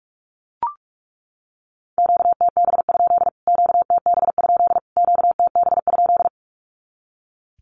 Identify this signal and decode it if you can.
Morse code, heard as a keyed tone, about 44 wpm, 700 Hz. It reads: QTB? QTB? QTB?
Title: Q-Codes Repeated 3x 45wpm